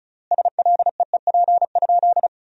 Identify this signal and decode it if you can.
Morse code, heard as a keyed tone, about 34 wpm, 700 Hz. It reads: SLEEP?
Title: 3x Word-Spacing - Sentences from Top 500 Words - 35wpm